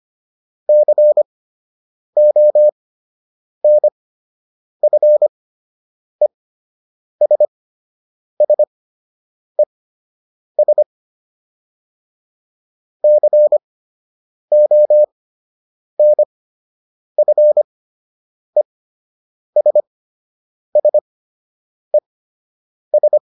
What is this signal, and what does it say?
Morse code, heard as a keyed tone, about 25 wpm, spaced as for 8 wpm, 600 Hz. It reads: CONFESSES CONFESSES